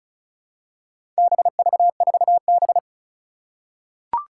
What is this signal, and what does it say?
Morse code, heard as a keyed tone, about 36 wpm, 700 Hz. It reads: BV4B
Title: Four Challenging Letter-Number Combinations 35wpm